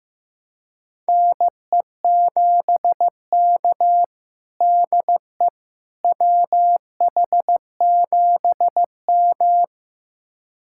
Morse code, heard as a keyed tone, about 16 wpm, 700 Hz. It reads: NE7K DE WH7M
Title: USA Callsign DE Callsign - 4 Characters - 15wpm